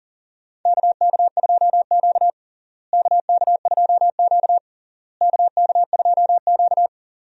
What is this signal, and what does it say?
Morse code, heard as a keyed tone, about 40 wpm, 700 Hz. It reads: KK2Q KK2Q KK2Q
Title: USA Call Signs - 4 Characters - Repeated 3x - 40wpm